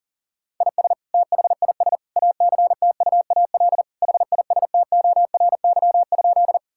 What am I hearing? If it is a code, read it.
Morse code, heard as a keyed tone, about 40 wpm, 700 Hz. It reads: IS THIS ACTUAL HISTORY?
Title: Sentences from Top 1000 Words - Review Part 5 - 40wpm